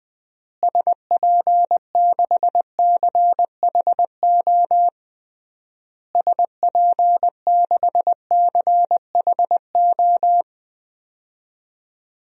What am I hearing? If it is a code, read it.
Morse code, heard as a keyed tone, about 20 wpm, 700 Hz. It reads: SP6CHO SP6CHO
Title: DXCC-100 Call Signs - 6 Characters - Repeated 3x - 20wpm